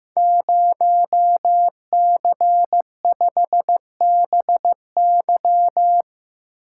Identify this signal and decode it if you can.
Morse code, heard as a keyed tone, about 15 wpm, 700 Hz. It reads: 0C5BY
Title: Code Groups - 5 Letter:Numbers - 15wpm